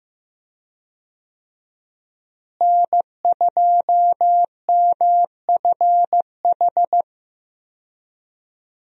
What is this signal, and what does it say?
Morse code, heard as a keyed tone, about 15 wpm, 700 Hz. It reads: N2MFH
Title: USA Call Signs - 5 Characters - Repeated 2x - 15wpm